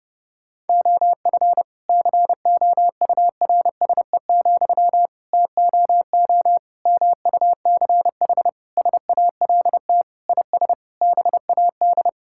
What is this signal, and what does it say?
Morse code, heard as a keyed tone, about 30 wpm, 700 Hz. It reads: OF COURSE, TOO MUCH SALT IS BAD
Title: Sentences from Top 1000 Words - Review Part 5 - 30wpm